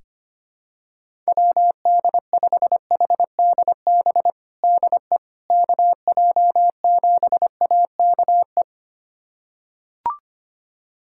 Morse code, heard as a keyed tone, about 25 wpm, 700 Hz. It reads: WD5HDB DE KJ7AKE
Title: USA Callsign DE Callsign - 6 Characters - 25wpm